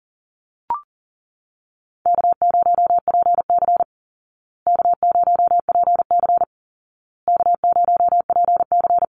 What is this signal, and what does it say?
Morse code, heard as a keyed tone, about 40 wpm, 700 Hz. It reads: K0PC K0PC K0PC
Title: Easy Call Signs Repeated 3x - 40wpm